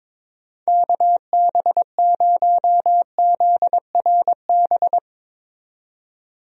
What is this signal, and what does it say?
Morse code, heard as a keyed tone, about 22 wpm, 700 Hz. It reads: KB0ZRB
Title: USA Call Signs - 6 Characters - 22wpm